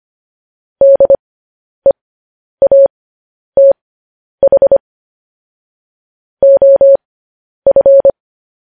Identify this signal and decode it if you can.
Morse code, heard as a keyed tone, about 25 wpm, spaced as for 10 wpm, 550 Hz. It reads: DEATH OF
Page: News Headlines in Morse Code at 10 WPM